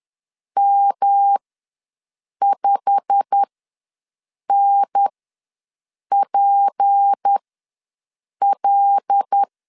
Morse code, heard as a keyed tone, about 10 wpm, spaced as for 6 wpm, 800 Hz. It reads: M5NPL
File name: LEZIONI / 9 2867 ws 10wpm 4215